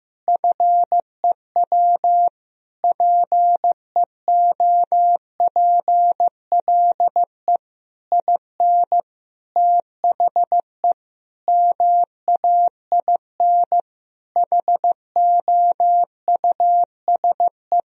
Morse code, heard as a keyed tone, about 15 wpm, 700 Hz. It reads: FEW PEOPLE IN THE MAIN HOUSE
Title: Sentences from Top 1000 Words - Review Part 2 - 15wpm